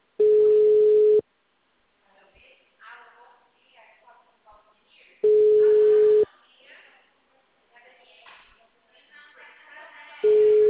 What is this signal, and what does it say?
Ringback tone (Europe, 425 Hz), with speech.
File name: out- -2044- - - .28261